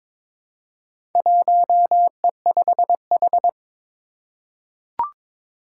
Morse code, heard as a keyed tone, about 22 wpm, 700 Hz. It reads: 1E5H
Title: Four Challenging Letter-Number Combinations 22wpm